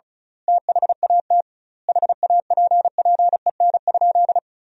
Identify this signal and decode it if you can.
Morse code, heard as a keyed tone, about 35 wpm, 700 Hz. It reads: THAT HAPPEN?
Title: 2x Word-Spacing - Sentences from Top 400 Words - 35wpm